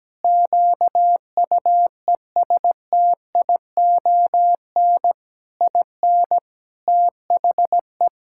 Morse code, heard as a keyed tone, about 17 wpm, 700 Hz. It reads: QUESTION IN THE
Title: Sentences from Top 1000 Words - Review Part 6 - 17wpm